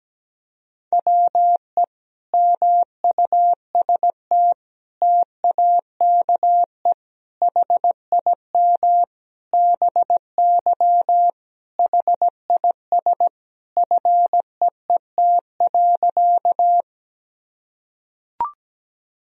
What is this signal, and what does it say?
Morse code, heard as a keyed tone, about 17 wpm, 700 Hz. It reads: WE MUST TAKE HIM BY HIS FEET.
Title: Sentences from Top 1000 Words - Review Part 1 - 17wpm